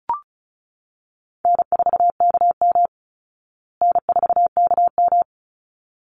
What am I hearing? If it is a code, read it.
Morse code, heard as a keyed tone, about 34 wpm, 700 Hz. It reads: N4KM N4KM
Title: Easy Call Signs Repeated 2x - 35wpm